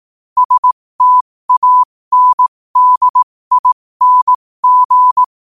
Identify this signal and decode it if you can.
Morse code, heard as a keyed tone, about 18 wpm, spaced as for 15 wpm, 1000 Hz. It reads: STANDING
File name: qotdfiles / 15wpm